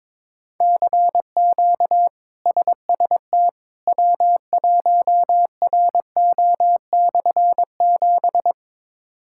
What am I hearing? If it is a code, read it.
Morse code, heard as a keyed tone, about 22 wpm, 700 Hz. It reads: CQ SST W1RO/7
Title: K1USN SST QSO Practice - 22wpm